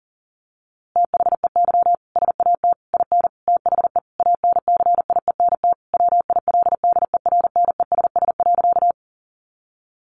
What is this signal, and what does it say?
Morse code, heard as a keyed tone, about 37 wpm, 700 Hz. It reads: THEY SAT IN THE ANCIENT WILDERNESS.
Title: A Sound Of Thunder 40wpm